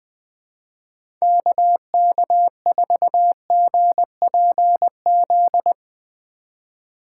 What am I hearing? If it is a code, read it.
Morse code, heard as a keyed tone, about 21 wpm, 700 Hz. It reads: KK4GPZ
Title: USA Call Signs - 6 Characters - 4x Character Spacing - 20wpm